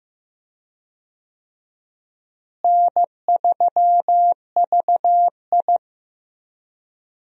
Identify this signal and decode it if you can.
Morse code, heard as a keyed tone, about 15 wpm, 700 Hz. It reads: N3VI
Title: USA Call Signs - 4 Characters - Repeated 2x - 15wpm